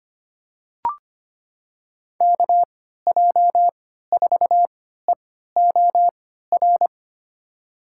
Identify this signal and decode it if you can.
Morse code, heard as a keyed tone, about 25 wpm, spaced as for 14 wpm, 700 Hz. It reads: KJ4EOR
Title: USA Call Signs - 6 Characters - 3x Character Spacing - 25wpm